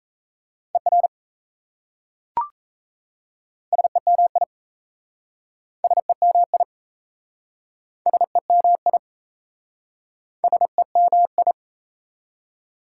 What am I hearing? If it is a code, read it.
Morse code, heard as a keyed tone, about 37 wpm, 690 Hz. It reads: ER SEMI SEMI SEMI SEMI